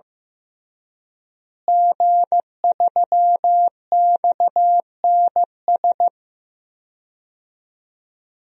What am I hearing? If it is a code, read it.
Morse code, heard as a keyed tone, about 16 wpm, 700 Hz. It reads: G3XNS
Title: DXCC-100 Call Signs - 5 Characters - Repeated 3x - 15wpm